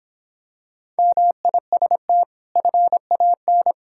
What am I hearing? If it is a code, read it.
Morse code, heard as a keyed tone, about 26 wpm, 700 Hz. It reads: MIST FAN